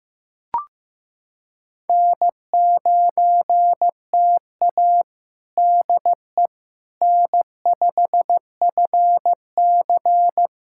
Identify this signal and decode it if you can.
Morse code, heard as a keyed tone, about 15 wpm, 700 Hz. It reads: N9TA DE N5FC
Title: USA Callsign DE Callsign - 4 Characters - 15wpm